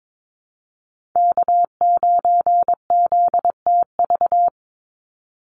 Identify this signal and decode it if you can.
Morse code, heard as a keyed tone, about 22 wpm, 700 Hz. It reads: K9ZTV
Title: Call Signs Repeated 2x - 22wpm